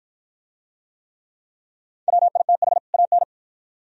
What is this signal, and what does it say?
Morse code, heard as a keyed tone, about 52 wpm, 700 Hz. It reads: WITH AN